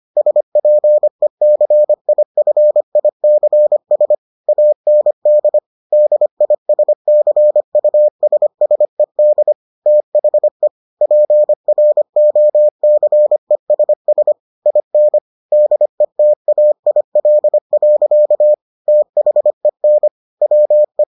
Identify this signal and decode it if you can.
Morse code, heard as a keyed tone, about 25 wpm, 600 Hz. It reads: SPECIFICS AND DISCUSSED THE PROCESS IN DETAIL. THEN WE